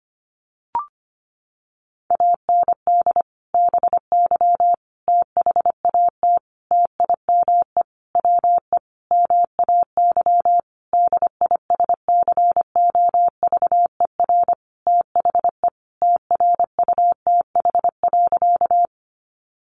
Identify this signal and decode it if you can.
Morse code, heard as a keyed tone, about 25 wpm, 700 Hz. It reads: AND BY THAT TIME WE MAY DISCOVER THE TRUTH.
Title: Sentences from Top 1100 Words 25wpm